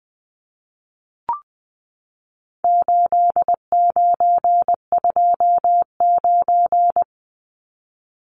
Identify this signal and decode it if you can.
Morse code, heard as a keyed tone, about 19 wpm, 700 Hz. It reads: 8929
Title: Numbers - 4-Digit 20wpm